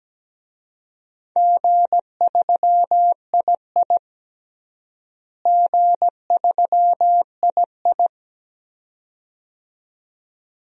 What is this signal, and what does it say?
Morse code, heard as a keyed tone, about 17 wpm, 700 Hz. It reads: G3II G3II
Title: DXCC-100 Call Signs - 4 Characters - Repeated 2x - 17wpm